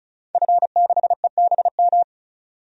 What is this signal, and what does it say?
Morse code, heard as a keyed tone, about 36 wpm, 700 Hz. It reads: F6EBM
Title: DXCC-100 Call Signs - 5 Characters - 35wpm